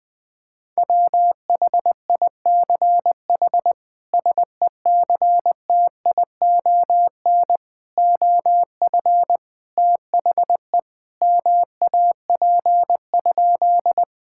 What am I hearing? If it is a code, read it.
Morse code, heard as a keyed tone, about 19 wpm, 700 Hz. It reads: WHICH SECTION OF THE MAP?
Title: Sentences from Top 1000 Words - Review Part 5 - 20wpm